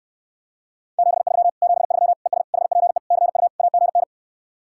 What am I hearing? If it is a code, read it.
Morse code, heard as a keyed tone, about 68 wpm, 700 Hz. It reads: 73 73 ES HPE CU AGN